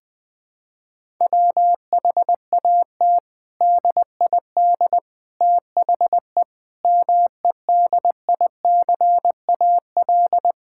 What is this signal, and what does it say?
Morse code, heard as a keyed tone, about 21 wpm, 700 Hz. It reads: WHAT DID THE MEDICAL
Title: Sentences from Top 1000 Words - Review Part 5 - 20wpm